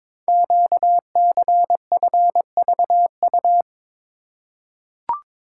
Morse code, heard as a keyed tone, about 22 wpm, 700 Hz. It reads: QCFVU
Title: Code Groups - 5 Letters - 22wpm